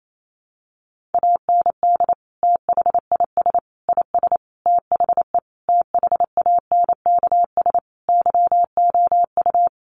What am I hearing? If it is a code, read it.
Morse code, heard as a keyed tone, about 28 wpm, 700 Hz. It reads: AND THIS IS THE THANKS YOU
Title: Sentences from Top 1200 Words 28wpm